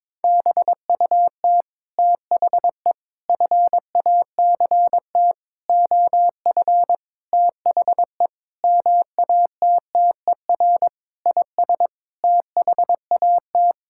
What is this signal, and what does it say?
Morse code, heard as a keyed tone, about 22 wpm, 700 Hz. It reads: BUT THE FACT OF THE MATTER IS THAT
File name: Sets of 9 Words - 22wpm